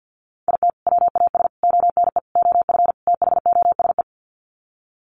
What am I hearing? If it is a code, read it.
Morse code, heard as a keyed tone, about 48 wpm, 700 Hz. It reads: IT WAS ONE OF THOSE
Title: Sets of 5 Words Encore 50wpm